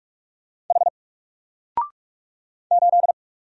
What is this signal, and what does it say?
Morse code, heard as a keyed tone, about 44 wpm, 700 Hz. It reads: H 8
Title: Single Letter-Number 45wpm